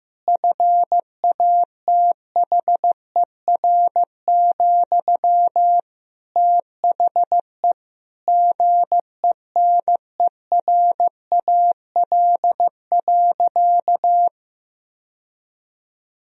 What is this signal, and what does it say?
Morse code, heard as a keyed tone, about 15 wpm, 700 Hz. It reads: FATHER, THE GENERAL.
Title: Sentences from Top 1000 Words - Review Part 3 - 15wpm